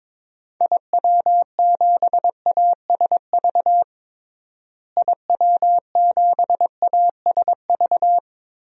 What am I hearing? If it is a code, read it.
Morse code, heard as a keyed tone, about 22 wpm, 700 Hz. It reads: IW7ASV IW7ASV
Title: DXCC-100 Call Signs - 6 Characters - Repeated 2x - 22wpm